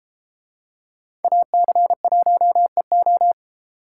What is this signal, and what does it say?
Morse code, heard as a keyed tone, about 33 wpm, 700 Hz. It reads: AC1EO